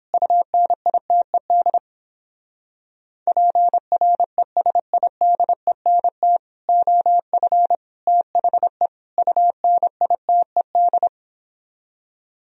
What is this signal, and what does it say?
Morse code, heard as a keyed tone, about 31 wpm, 700 Hz. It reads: UNITED PRESIDENT OF THE UNITED